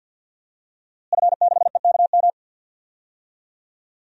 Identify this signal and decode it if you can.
Morse code, heard as a keyed tone, about 49 wpm, 700 Hz. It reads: F6EKM
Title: DXCC-100 Call Signs - 5 Characters - Repeated 2x - 50wpm